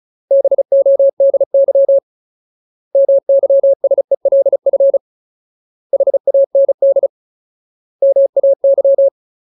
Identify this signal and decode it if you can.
Morse code, heard as a keyed tone, about 34 wpm, 550 Hz. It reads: BODY MYSELF HAND MAY
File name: Ultimate Word List / 35WPM Ultimate